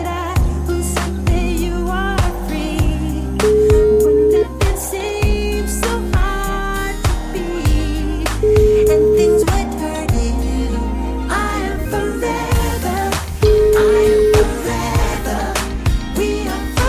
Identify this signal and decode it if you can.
Ringback tone (Europe, 425 Hz), with noise.